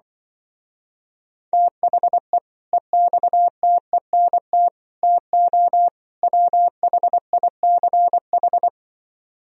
Morse code, heard as a keyed tone, about 25 wpm, 700 Hz. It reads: THE EXTENT TO WHICH
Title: Speed-Racing - Sets of 4 Words 20wpm